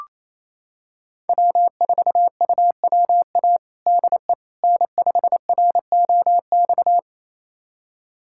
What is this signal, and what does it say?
Morse code, heard as a keyed tone, about 28 wpm, 700 Hz. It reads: W4UWA DE N5ROX
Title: USA Callsign DE Callsign - 5 Characters - 28wpm